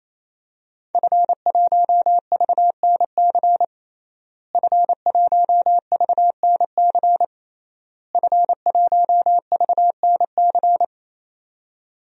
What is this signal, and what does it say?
Morse code, heard as a keyed tone, about 28 wpm, 700 Hz. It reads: F1VNC F1VNC F1VNC
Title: DXCC-100 Call Signs - 5 Characters - Repeated 3x - 28wpm